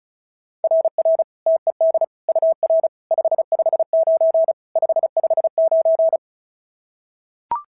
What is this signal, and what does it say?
Morse code, heard as a keyed tone, about 36 wpm, 690 Hz. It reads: RR TED UR 559 559
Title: SOTA QSO Practice 35wpm